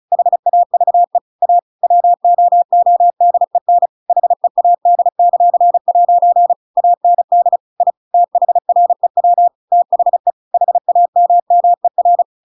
Morse code, heard as a keyed tone, about 34 wpm, 700 Hz. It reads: HAVE A WOODEN HEAD;' AND I THREW THE HAMMER
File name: Pinocchio / Pinocchio ch7 35 35 ik1hge